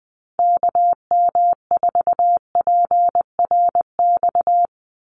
Morse code, heard as a keyed tone, about 20 wpm, 700 Hz. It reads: KM4PRX
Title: USA Call Signs 20wpm